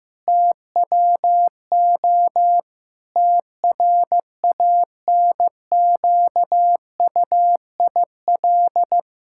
Morse code, heard as a keyed tone, about 16 wpm, 700 Hz. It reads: TWO TRANQUIL
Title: Pangram Challenge v2 - 15wpm